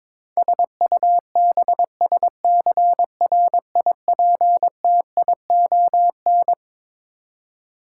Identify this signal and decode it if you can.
Morse code, heard as a keyed tone, about 22 wpm, 700 Hz. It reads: SUBSCRIPTION
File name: Twelve Letter Words - 22wpm